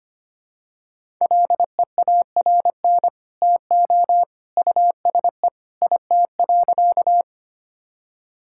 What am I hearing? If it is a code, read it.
Morse code, heard as a keyed tone, about 25 wpm, 700 Hz. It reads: LEARN TO USE IT.
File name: practice / 25wpm / Sentences with 2-Word Phrases - 25wpm